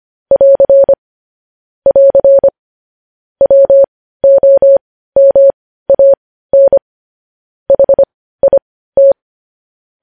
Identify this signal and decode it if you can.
Morse code, heard as a keyed tone, about 25 wpm, spaced as for 15 wpm, 550 Hz. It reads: <AR> <AR> WOMAN HIT
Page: news-headlines-in-morse-code-at-15-wpm